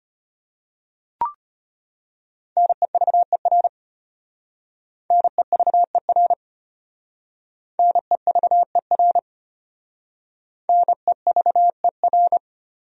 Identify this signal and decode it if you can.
Morse code, heard as a keyed tone, about 33 wpm, 700 Hz. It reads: NEVER NEVER NEVER NEVER